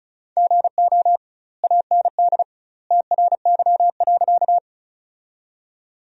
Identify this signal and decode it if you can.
Morse code, heard as a keyed tone, about 34 wpm, 700 Hz. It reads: GO AND TRY.
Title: 2x Word-Spacing - Sentences from Top 200 Words - 35wpm